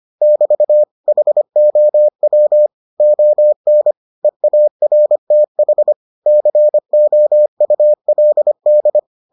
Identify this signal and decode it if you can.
Morse code, heard as a keyed tone, about 25 wpm, 600 Hz. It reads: <BT> HOW ON EARTH COULD